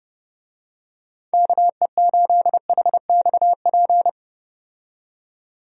Morse code, heard as a keyed tone, about 30 wpm, 700 Hz. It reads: KE8HXP
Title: USA Call Signs - 6 Characters - 30wpm